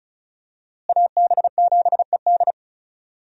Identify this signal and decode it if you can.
Morse code, heard as a keyed tone, about 36 wpm, 700 Hz. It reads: AB7ED
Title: USA Call Signs - 5 Characters - Repeated 3x - 35wpm